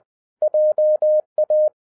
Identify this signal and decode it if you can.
Morse code, heard as a keyed tone, about 20 wpm, 610 Hz. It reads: JA